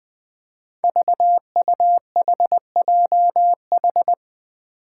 Morse code, heard as a keyed tone, about 20 wpm, 700 Hz. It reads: VUHJH
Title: Five Challenging Letter-Number Combinations 20wpm